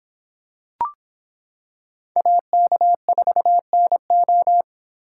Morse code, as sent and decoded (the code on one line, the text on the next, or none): .- -.- ....- -. ---
AK4NO